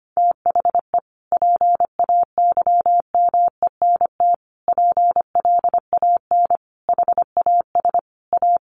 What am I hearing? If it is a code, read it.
Morse code, heard as a keyed tone, about 25 wpm, 700 Hz. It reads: THE PAYMENT PLAN HAS A
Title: Sentences from Top 1500 Words 25wpm